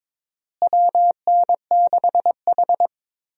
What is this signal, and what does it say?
Morse code, heard as a keyed tone, about 22 wpm, 700 Hz. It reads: WN6H
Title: USA Call Signs - 4 Characters - Repeated 2x - 22wpm